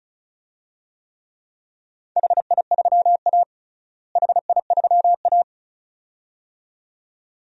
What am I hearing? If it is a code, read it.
Morse code, heard as a keyed tone, about 34 wpm, 700 Hz. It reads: HI3A HI3A
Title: DXCC-100 Call Signs - 4 Characters - Repeated 2x - 35wpm